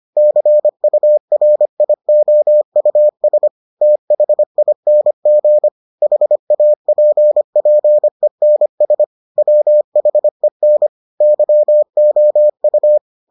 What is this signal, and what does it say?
Morse code, heard as a keyed tone, about 25 wpm, 600 Hz. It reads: CURIOUS THING HAPPENS WHEN YOU